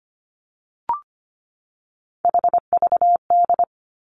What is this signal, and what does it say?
Morse code, heard as a keyed tone, about 25 wpm, 700 Hz. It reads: HVD
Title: Three Challenging Letter-Number Combinations 25wpm